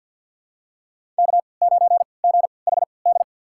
Morse code, heard as a keyed tone, about 52 wpm, spaced as for 29 wpm, 690 Hz. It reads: K9GHD
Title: USA Call Signs - 3x Character Spacing - 50wpm